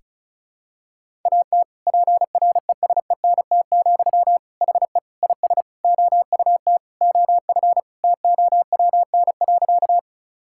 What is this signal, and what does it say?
Morse code, heard as a keyed tone, about 34 wpm, 700 Hz. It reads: AT PRESENT, HE IS OUT OF TOWN.
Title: Sentences from Top 1000 Words - Review Part 5 - 35wpm